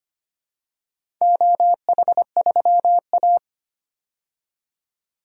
Morse code, heard as a keyed tone, about 25 wpm, 700 Hz. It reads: OH3A
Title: DXCC-100 Call Signs - 4 Characters - 25wpm